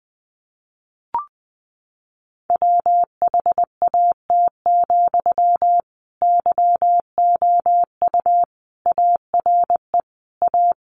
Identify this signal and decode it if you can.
Morse code, heard as a keyed tone, about 20 wpm, 700 Hz. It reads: WHAT, YOU ARE A
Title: Sentences from Top 1300 Words 20wpm